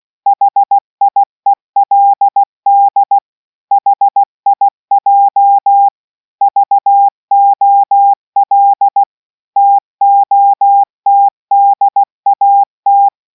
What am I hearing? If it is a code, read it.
Morse code, heard as a keyed tone, about 16 wpm, 800 Hz. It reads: HIELD HIJ VOL TOTDAT